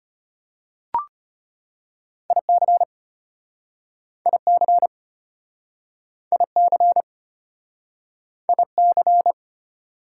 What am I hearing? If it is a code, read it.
Morse code, heard as a keyed tone, about 33 wpm, 700 Hz. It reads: IC IC IC IC